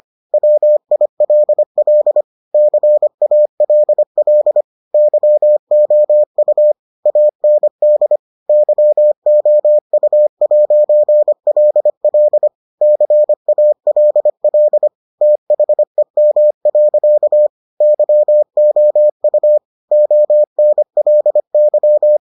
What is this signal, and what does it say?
Morse code, heard as a keyed tone, about 25 wpm, 600 Hz. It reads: WILL CALL YOU AND YOU'LL CALL THEM. YOU ONLY